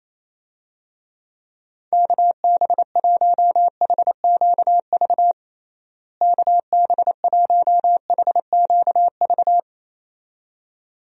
Morse code, heard as a keyed tone, about 27 wpm, 700 Hz. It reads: KB1HQV KB1HQV
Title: USA Call Signs - 6 Characters - Repeated 2x - 28wpm